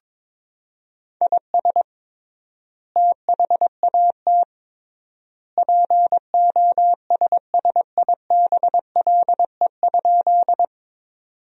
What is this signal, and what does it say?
Morse code, heard as a keyed tone, about 22 wpm, 700 Hz. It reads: IS THAT POSSIBLE?
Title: 3x Word-Spacing - Sentences from Top 500 Words - 22wpm